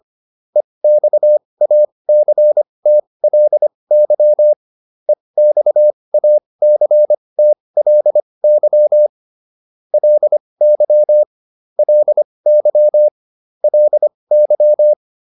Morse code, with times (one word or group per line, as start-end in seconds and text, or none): 0.6-4.5 EXACTLY
5.1-9.1 EXACTLY
9.9-11.2 LY
11.8-13.1 LY
13.6-14.9 LY